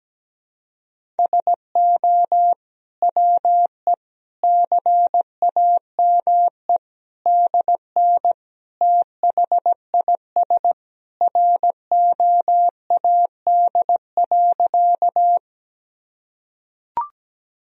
Morse code, heard as a keyed tone, about 17 wpm, 700 Hz. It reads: SO WE CAME DN THIS ROAD.
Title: Sentences from Top 400 Words and Abbreviations 17wpm